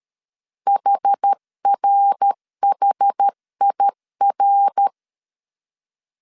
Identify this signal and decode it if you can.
Morse code, heard as a keyed tone, about 13 wpm, 800 Hz. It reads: HRHIR
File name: CW / DET / 1 3 1500C S A 12wpm 2905